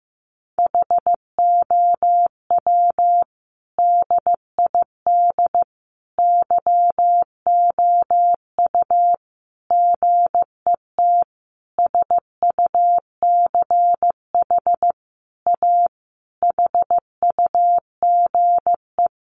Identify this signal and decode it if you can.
Morse code, heard as a keyed tone, about 15 wpm, 700 Hz. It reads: HOW DID YOU GET SUCH A HUGE
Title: Sentences from Top 1100 Words 15wpm